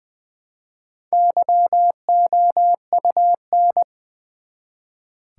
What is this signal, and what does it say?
Morse code, heard as a keyed tone, about 20 wpm, 700 Hz. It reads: YOUN
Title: Build A Word - 1 Letter At A Time - 5-Letter Words - 20wpm